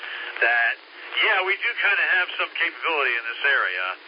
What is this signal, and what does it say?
Emergency Alert System (SAME) tones, heard as harsh buzzing bursts, with speech. It is heard from 0.3 to 3.8 s.